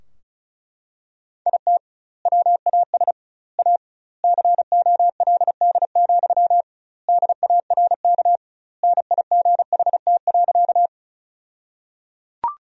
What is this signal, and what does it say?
Morse code, heard as a keyed tone, about 36 wpm, 700 Hz. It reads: IT WAS A COLD, DARK NIGHT.
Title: 2x Word-Spacing - Sentences from Top 500 Words - 35wpm